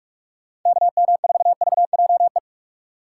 Morse code, heard as a keyed tone, about 41 wpm, 690 Hz. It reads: KM4VJE
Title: USA Call Signs - 4x Character Spacing - 45wpm